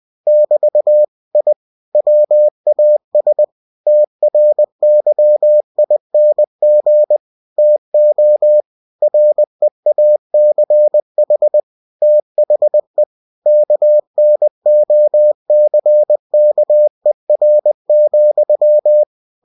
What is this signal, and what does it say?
Morse code, heard as a keyed tone, about 20 wpm, 600 Hz. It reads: <BT> I WAS TRYING TO REACH THE KNOCKER,
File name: cw-books / winnie-the-pooh / winnie-20wpm-page-0048